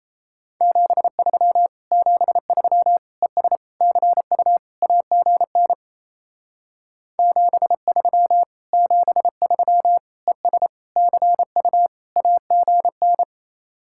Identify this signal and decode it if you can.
Morse code, heard as a keyed tone, about 34 wpm, 700 Hz. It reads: 73 73 ES CU AGN 73 73 ES CU AGN
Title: Speed-Racing Common QSO Phrases - 28wpm